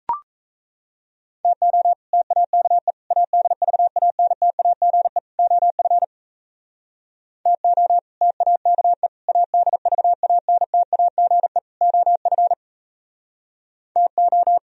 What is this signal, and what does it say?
Morse code, heard as a keyed tone, about 36 wpm, 690 Hz. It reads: TO TAKE ADVANTAGE OF TO TAKE ADVANTAGE OF TO